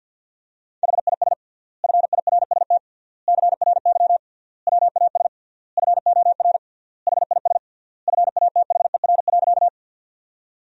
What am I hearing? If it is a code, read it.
Morse code, heard as a keyed tone, about 52 wpm, 700 Hz. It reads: HIS FIRST CRY WAS FOR HIS FATHER.
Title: 3x Word-Spacing - Sentences from Top 500 Words - 50wpm